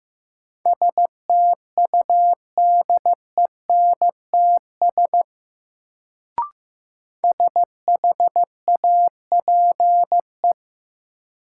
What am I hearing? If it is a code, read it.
Morse code, heard as a keyed tone, about 15 wpm, 700 Hz. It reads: STUDENTS SHAPE
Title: Sets of 5 Words Encore - New Word Drill - 15wpm